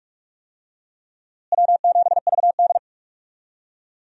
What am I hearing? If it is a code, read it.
Morse code, heard as a keyed tone, about 46 wpm, 700 Hz. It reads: W7VD
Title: USA Call Signs - 4 Characters - Repeated 2x - 45wpm